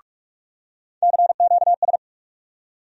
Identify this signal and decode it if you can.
Morse code, heard as a keyed tone, about 46 wpm, 700 Hz. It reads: CQS
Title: USA Call Sign Postfixes - 3 Letters 45wpm